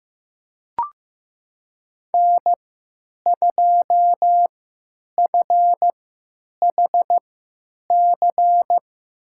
Morse code, heard as a keyed tone, about 15 wpm, spaced as for 9 wpm, 700 Hz. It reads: N2FHC
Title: USA Call Signs - 5 Characters - 3x Character Spacing - 15wpm